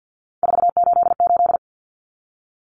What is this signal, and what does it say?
Morse code, heard as a keyed tone, about 50 wpm, 700 Hz. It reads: 488
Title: Numbers - 3-Digit 50wpm